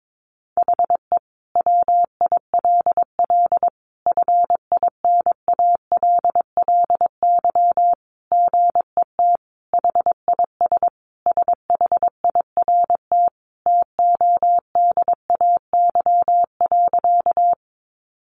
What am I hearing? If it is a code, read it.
Morse code, heard as a keyed tone, about 22 wpm, 700 Hz. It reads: HE WILL FINALLY GET HIS SHIRT TODAY.
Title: Additional Easy Phrases 22wpm